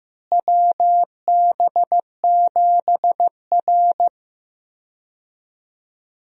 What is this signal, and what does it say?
Morse code, heard as a keyed tone, about 15 wpm, 700 Hz. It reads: WB7R
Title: USA Call Signs - 4 Characters - Repeated 2x - 15wpm